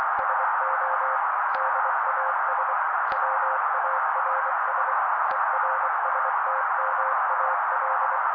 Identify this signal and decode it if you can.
Morse code, heard as a keyed tone, about 24 wpm, 530 Hz. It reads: SO DAS WARS ERSTMAL
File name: daily / 23wpm noise